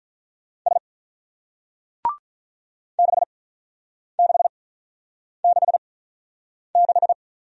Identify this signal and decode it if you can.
Morse code, heard as a keyed tone, about 50 wpm, spaced as for 9 wpm, 700 Hz. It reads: S 6666